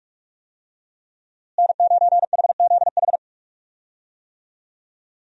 Morse code, heard as a keyed tone, about 46 wpm, 700 Hz. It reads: N9HZH
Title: USA Call Signs - 5 Characters - Repeated 3x - 45wpm